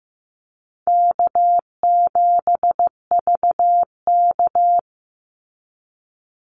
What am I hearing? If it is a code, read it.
Morse code, heard as a keyed tone, about 15 wpm, 700 Hz. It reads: K7VK
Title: NA SOTA Activator Callsigns 15wpm